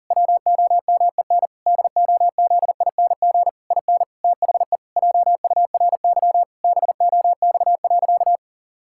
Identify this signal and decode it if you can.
Morse code, heard as a keyed tone, about 40 wpm, 700 Hz. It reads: WOMEN DOZING IN THE JURY BOX.
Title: Pangram Challenge v2 - 40wpm